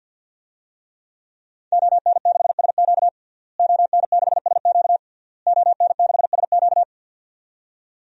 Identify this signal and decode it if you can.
Morse code, heard as a keyed tone, about 50 wpm, 700 Hz. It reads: ON6SQ ON6SQ ON6SQ
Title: DXCC-100 Call Signs - 5 Characters - Repeated 3x - 50wpm